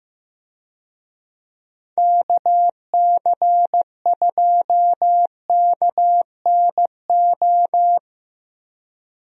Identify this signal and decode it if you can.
Morse code, heard as a keyed tone, about 16 wpm, 700 Hz. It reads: KC2KNO
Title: USA Call Signs - 6 Characters - Repeated 2x - 15wpm